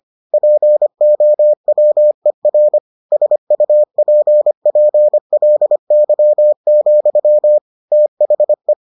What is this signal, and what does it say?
Morse code, heard as a keyed tone, about 25 wpm, 600 Hz. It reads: POWER SUPPLY, THE